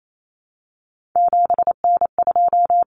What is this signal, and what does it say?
Morse code, heard as a keyed tone, about 28 wpm, 700 Hz. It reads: 7N2
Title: Common Cut Numbers - 3-Digit 28wpm